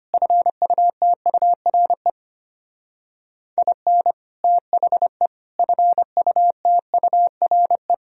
Morse code, heard as a keyed tone, about 29 wpm, 700 Hz. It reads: FUTURE IN THE FUTURE
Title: Speed-Racing - Sets of 3 Words 25wpm